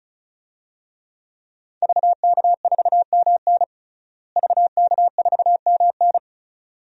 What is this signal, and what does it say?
Morse code, heard as a keyed tone, about 36 wpm, 700 Hz. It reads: VK4MN VK4MN
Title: DXCC-100 Call Signs - 5 Characters - Repeated 2x - 35wpm